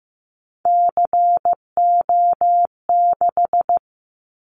Morse code, heard as a keyed tone, about 15 wpm, 700 Hz. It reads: CO6
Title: Callsign Prefixes for Top 100 DXCC - 15wpm